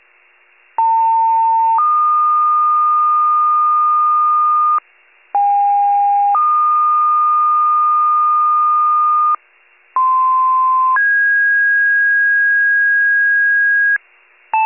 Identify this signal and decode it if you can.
Two-tone page, heard as a one-second tone, then a three-second tone, about 900 then 1220 Hz, about 810 then 1220 Hz, about 1040 then 1650 Hz, with noise. It is heard from 0.8 to 14.0 s.
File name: pager / 95 RESCUE DUTY / wcalert